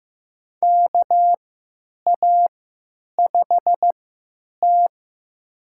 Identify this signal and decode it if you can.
Morse code, heard as a keyed tone, about 15 wpm, spaced as for 9 wpm, 700 Hz. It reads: KA5T
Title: USA Call Signs - 3x Character Spacing - 15wpm